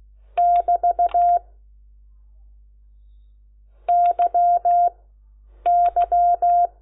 Morse code, heard as a keyed tone, about 16 wpm, 700 Hz. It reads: <BT> Y Y
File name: Lecciones / CLASE Nº10